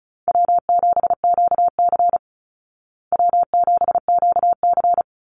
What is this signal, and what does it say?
Morse code, heard as a keyed tone, about 34 wpm, 700 Hz. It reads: W7QC W7QC
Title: Easy Call Signs Repeated 2x - 35wpm